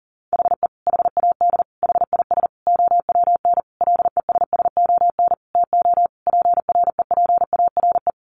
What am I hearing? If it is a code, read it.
Morse code, heard as a keyed tone, about 40 wpm, 700 Hz. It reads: HE HAD HIS OWN LESSON TO PREPARE
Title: Sentences from Top 1200 Words 40wpm